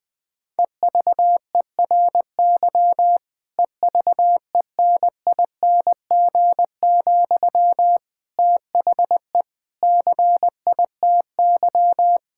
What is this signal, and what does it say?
Morse code, heard as a keyed tone, about 21 wpm, 700 Hz. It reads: EVERY EVENING, THE CITY
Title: Difficult Sentences with Hints - 20wpm